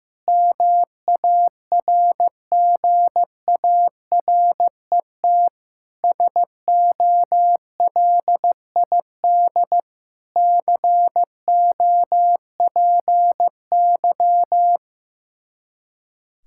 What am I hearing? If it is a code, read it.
Morse code, heard as a keyed tone, about 16 wpm, 700 Hz. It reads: MARGARET SOLID COPY
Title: Practice QSO - Exchange 3 - More Info - 15wpm